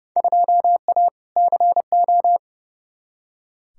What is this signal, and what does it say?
Morse code, heard as a keyed tone, about 32 wpm, 700 Hz. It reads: 2A CO
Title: Field Day Exchanges - 30wpm